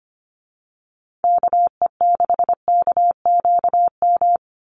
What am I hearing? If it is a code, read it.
Morse code, heard as a keyed tone, about 25 wpm, 700 Hz. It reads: KE6KQM
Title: USA Call Signs 25wpm